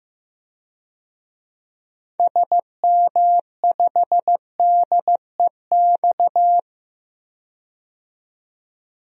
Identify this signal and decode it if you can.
Morse code, heard as a keyed tone, about 15 wpm, 700 Hz. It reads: SM5DEX
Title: DXCC-100 Call Signs - 6 Characters - 15wpm